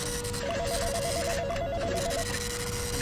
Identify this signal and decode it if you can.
There is no code.